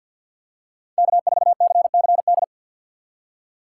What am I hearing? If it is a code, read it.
Morse code, heard as a keyed tone, about 48 wpm, 700 Hz. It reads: K4CCD
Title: USA Call Signs - 5 Characters - 3x Character Spacing - 50wpm